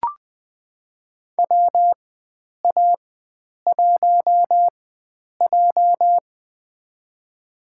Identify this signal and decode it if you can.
Morse code, heard as a keyed tone, about 20 wpm, spaced as for 9 wpm, 700 Hz. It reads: WA1J
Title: USA Call Signs - 4x Character Spacing - 20wpm